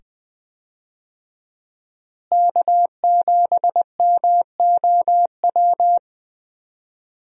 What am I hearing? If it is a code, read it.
Morse code, heard as a keyed tone, about 20 wpm, 700 Hz. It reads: K7MOW
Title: USA Call Signs - 5 Characters - Repeated 2x - 20wpm